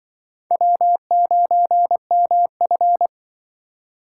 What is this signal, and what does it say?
Morse code, heard as a keyed tone, about 25 wpm, 700 Hz. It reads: W9MF
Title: Speed Racing - USA 1x2 Call Signs 20wpm